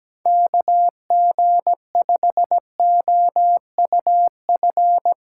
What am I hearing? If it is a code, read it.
Morse code, heard as a keyed tone, about 17 wpm, 700 Hz. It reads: KG5OUF
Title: USA Callsign DE Callsign 17wpm